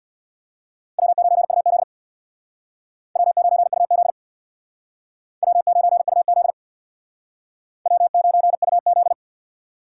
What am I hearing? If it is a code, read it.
Morse code, heard as a keyed tone, about 62 wpm, 700 Hz. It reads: W9UZ W9UZ W9UZ W9UZ
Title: Speed Racing - USA Call Signs 50wpm